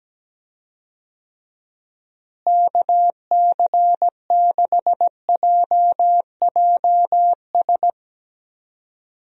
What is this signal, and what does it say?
Morse code, heard as a keyed tone, about 17 wpm, 700 Hz. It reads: KC6JJS
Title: USA Call Signs - 6 Characters - Repeated 2x - 17wpm